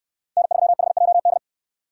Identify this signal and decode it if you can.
Morse code, heard as a keyed tone, about 67 wpm, 700 Hz. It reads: N3HJD